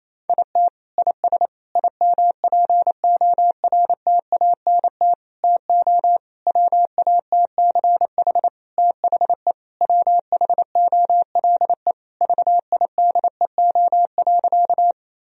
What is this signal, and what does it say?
Morse code, heard as a keyed tone, about 28 wpm, 700 Hz. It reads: IT IS IMPORTANT TO WATCH THE WHOLE VIDEO.
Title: Sentences from Top 1000 Words - Review Part 4 - 28wpm